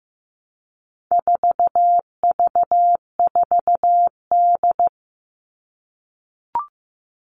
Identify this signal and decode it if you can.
Morse code, heard as a keyed tone, about 15 wpm, 700 Hz. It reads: 4V4D
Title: All Cut Numbers - 4-Digit 15wpm